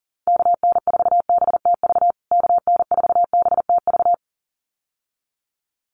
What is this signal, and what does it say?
Morse code, heard as a keyed tone, about 37 wpm, 700 Hz. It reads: KN4BTV KN4BTV
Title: USA Call Signs Repeated 2x 40wpm